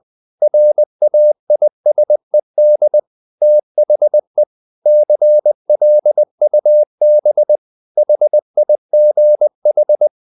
Morse code, heard as a keyed tone, about 20 wpm, 600 Hz. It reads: RAISED THE CLUB HIGH
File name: the old man and the sea part 2 20wpm